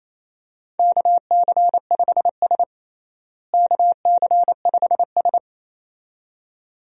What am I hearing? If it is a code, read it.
Morse code, heard as a keyed tone, about 27 wpm, 700 Hz. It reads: KC5S KC5S
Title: USA Call Signs - 4 Characters - Repeated 2x - 28wpm